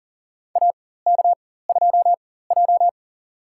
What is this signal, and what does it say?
Morse code, heard as a keyed tone, about 40 wpm, spaced as for 19 wpm, 700 Hz. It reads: AK2J
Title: USA Call Signs - 4 Characters - 4x Character Spacing - 40wpm